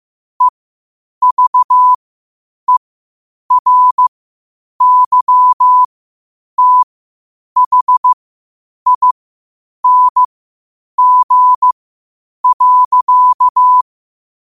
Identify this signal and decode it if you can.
Morse code, heard as a keyed tone, about 15 wpm, spaced as for 8 wpm, 1000 Hz. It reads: EVERYTHING.
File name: qotdfiles / 8wpm